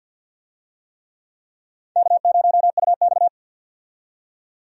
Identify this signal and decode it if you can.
Morse code, heard as a keyed tone, about 52 wpm, 700 Hz. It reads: K0UX
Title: USA Call Signs - 4 Characters - 50wpm